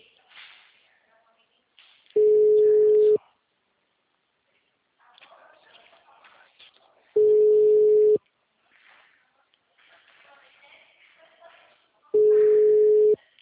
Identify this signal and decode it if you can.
Ringback tone (Europe, 425 Hz), with speech.